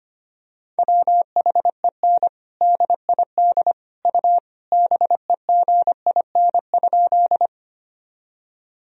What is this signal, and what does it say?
Morse code, heard as a keyed tone, about 25 wpm, 700 Hz. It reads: WHEN DID U BEGIN?
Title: Sentences from Top 300 Words and Abbreviations 25wpm